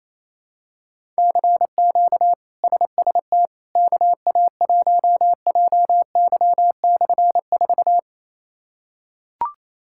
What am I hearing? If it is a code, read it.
Morse code, heard as a keyed tone, about 28 wpm, 700 Hz. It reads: CQ SST KA1JY/4
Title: K1USN SST QSO Practice - 28wpm